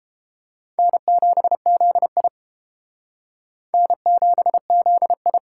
Morse code, heard as a keyed tone, about 29 wpm, 700 Hz. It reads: N7ZI N7ZI